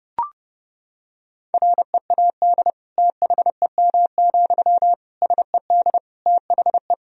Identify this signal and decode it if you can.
Morse code, heard as a keyed tone, about 31 wpm, 700 Hz. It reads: READ THEM, SED THE
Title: Sentences from Top 400 Words and Abbreviations 30wpm